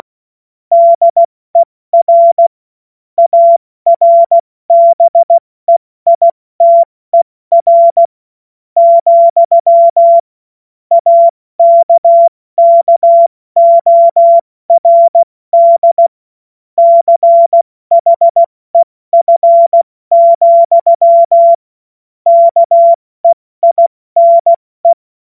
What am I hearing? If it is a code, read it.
Morse code, heard as a keyed tone, about 16 wpm, spaced as for 14 wpm, 680 Hz. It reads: DER ARBEITER, AKKORD CHEF, KEINE
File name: Morsekurs / Witze / Witz 27 Februar 2024 70 F80